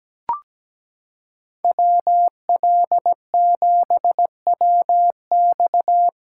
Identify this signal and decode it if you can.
Morse code, heard as a keyed tone, about 17 wpm, 700 Hz. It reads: WL7WX
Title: USA Callsign DE Callsign 17wpm